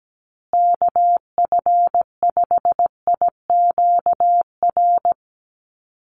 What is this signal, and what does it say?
Morse code, heard as a keyed tone, about 17 wpm, 700 Hz. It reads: KF5IQR
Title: USA Call Signs Repeated 3x 17wpm